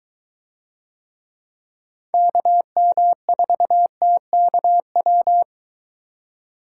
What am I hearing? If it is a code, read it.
Morse code, heard as a keyed tone, about 21 wpm, 700 Hz. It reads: KM4TKW